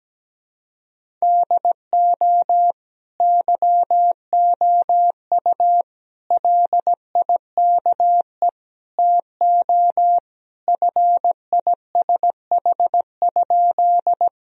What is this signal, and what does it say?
Morse code, heard as a keyed tone, about 17 wpm, 700 Hz. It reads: DO YOU LIKE TO FISH?
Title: Sentences from Top 1000 Words - Review Part 1 - 17wpm